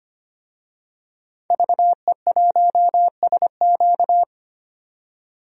Morse code, heard as a keyed tone, about 25 wpm, 700 Hz. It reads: VE1SQ
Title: DXCC-100 Call Signs - 5 Characters - 25wpm